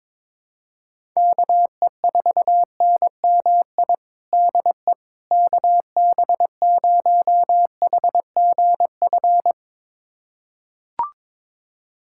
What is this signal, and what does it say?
Morse code, heard as a keyed tone, about 21 wpm, 700 Hz. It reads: KE4NMI DE KB0HGF
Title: USA Callsign DE Callsign - 6 Characters - 22wpm